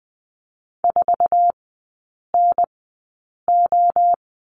Morse code, heard as a keyed tone, about 20 wpm, spaced as for 8 wpm, 700 Hz. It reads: 4NO
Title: 047 Lesson 9e - Number 4 - Three Characters 20wpm